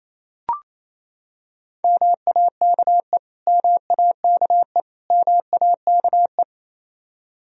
Morse code, heard as a keyed tone, about 28 wpm, 700 Hz. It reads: MAKE MAKE MAKE
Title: Top 100 Words Repeated 28wpm